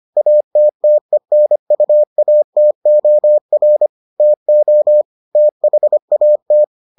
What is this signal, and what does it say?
Morse code, heard as a keyed tone, about 25 wpm, 600 Hz. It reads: ATTENUATOR TO THAT